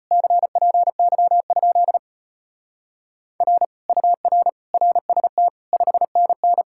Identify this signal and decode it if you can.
Morse code, heard as a keyed tone, about 33 wpm, 700 Hz. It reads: CPY? R UR RST 5NN